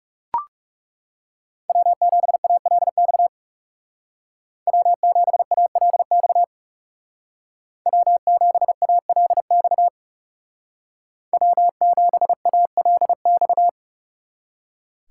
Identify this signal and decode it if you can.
Morse code, heard as a keyed tone, about 37 wpm, 700 Hz. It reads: W7ALX W7ALX W7ALX W7ALX
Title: Speed Racing - USA Call Signs 30wpm